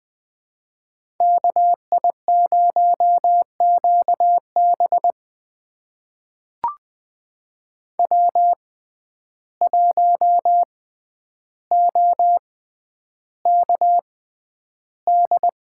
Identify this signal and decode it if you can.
Morse code, heard as a keyed tone, about 20 wpm, spaced as for 10 wpm, 700 Hz. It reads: KI0QB W1OKD
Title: USA Call Signs - 5 Characters - 6x Character Spacing - 20wpm